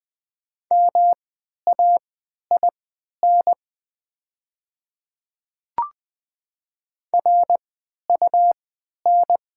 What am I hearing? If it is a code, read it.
Morse code, heard as a keyed tone, about 20 wpm, spaced as for 11 wpm, 700 Hz. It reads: MAIN RUN
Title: 3x Character Spacing - Top 300 Words - 20wpm